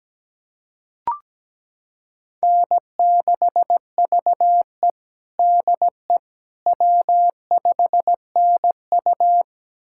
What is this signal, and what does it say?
Morse code, heard as a keyed tone, about 17 wpm, 700 Hz. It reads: N6VE DE W5NU
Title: USA Callsign DE Callsign - 4 Characters - 17wpm